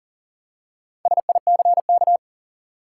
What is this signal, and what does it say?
Morse code, heard as a keyed tone, about 38 wpm, 700 Hz. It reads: SICK
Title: Four Letter Words - 40wpm